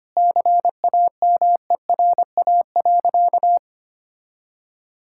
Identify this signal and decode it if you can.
Morse code, heard as a keyed tone, about 25 wpm, 700 Hz. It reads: CAMERA.
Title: Sentences from Top 1000 Words - Review Part 5 - 25wpm